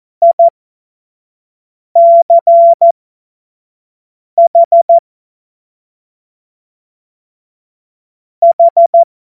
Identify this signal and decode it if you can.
Morse code, heard as a keyed tone, about 14 wpm, spaced as for 5 wpm, 680 Hz. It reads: ICH H
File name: Morsekurs / Witze / Witz 20. Januar 2026 25 F70